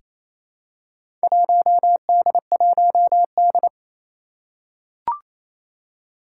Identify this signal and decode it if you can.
Morse code, heard as a keyed tone, about 28 wpm, 700 Hz. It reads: 1D1D
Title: Four Challenging Letter-Number Combinations 28wpm